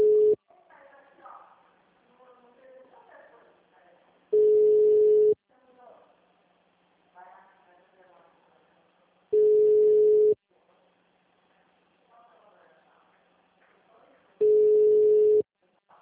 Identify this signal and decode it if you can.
Ringback tone (Europe, 425 Hz), with speech.